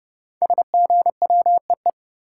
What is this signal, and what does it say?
Morse code, heard as a keyed tone, about 30 wpm, 700 Hz. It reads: SGWEE
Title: Code Groups - 5 Letters - 30wpm